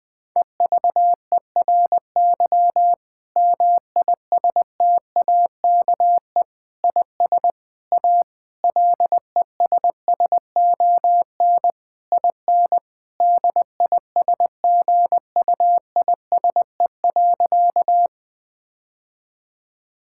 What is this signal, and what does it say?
Morse code, heard as a keyed tone, about 21 wpm, 700 Hz. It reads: EVERY MISTAKE IS A LESSON IN DISGUISE.
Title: Difficult Sentences with Hints - 20wpm